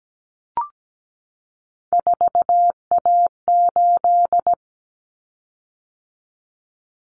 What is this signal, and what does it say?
Morse code, heard as a keyed tone, about 17 wpm, 700 Hz. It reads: 4A8
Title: Common Cut Numbers - 3-Digit 17wpm